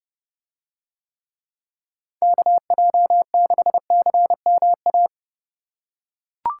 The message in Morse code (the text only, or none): KJ6CMA